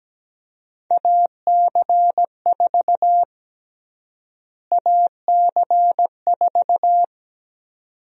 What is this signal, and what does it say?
Morse code, heard as a keyed tone, about 17 wpm, 700 Hz. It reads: AC4 AC4
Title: USA Call Sign Prefixes - Repeated 2x - 17wpm